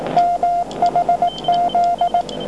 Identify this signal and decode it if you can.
Morse code, heard as a keyed tone, about 18 wpm, 700 Hz, with noise. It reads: MHZ